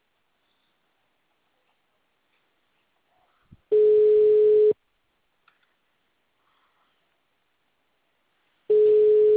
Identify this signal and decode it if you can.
Ringback tone (Europe, 425 Hz).
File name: out- -2004- - - .97682